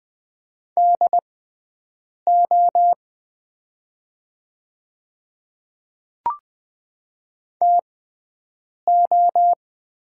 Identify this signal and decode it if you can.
Morse code, heard as a keyed tone, about 20 wpm, spaced as for 7 wpm, 700 Hz. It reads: DO TO
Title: 6x Character Spacing - Top 100 Words - 20wpm